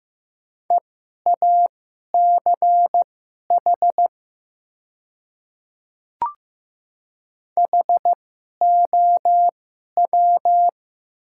Morse code, heard as a keyed tone, about 15 wpm, spaced as for 11 wpm, 700 Hz. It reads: EACH HOW
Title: 2x Character Spacing - Top 100 Words - 15wpm